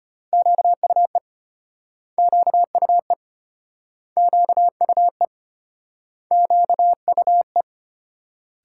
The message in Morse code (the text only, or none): QUE QUE QUE QUE